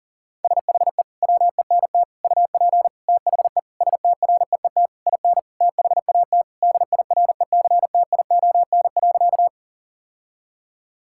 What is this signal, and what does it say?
Morse code, heard as a keyed tone, about 40 wpm, 700 Hz. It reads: SHE WENT UP THE STREET IN THAT DIRECTION.
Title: Sentences from Top 1000 Words - Review Part 5 - 40wpm